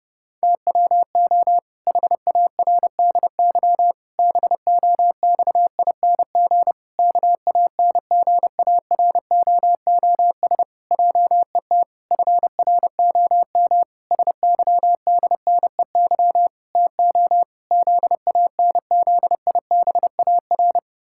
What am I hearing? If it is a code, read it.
Morse code, heard as a keyed tone, about 30 wpm, 700 Hz. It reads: TWO HARDY BOXING KANGAROOS JET FROM SYDNEY TO ZANZIBAR
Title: Pangram Challenge v2 - 30wpm